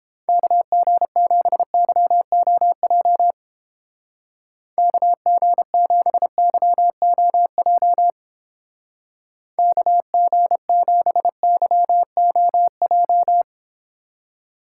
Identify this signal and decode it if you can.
Morse code, heard as a keyed tone, about 32 wpm, 700 Hz. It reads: KG7YOJ KG7YOJ KG7YOJ